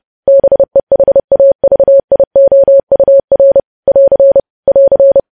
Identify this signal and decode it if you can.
Morse code, heard as a keyed tone, about 30 wpm, 550 Hz. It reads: BEHAVIOUR <AR> <AR>
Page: news headlines in morse code at 30 wpm